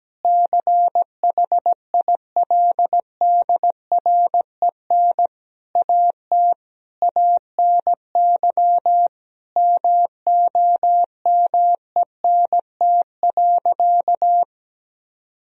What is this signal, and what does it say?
Morse code, heard as a keyed tone, about 17 wpm, 700 Hz. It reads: CHILDREN AT ANY MOMENT.
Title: Sentences from Top 1000 Words - Review Part 3 - 17wpm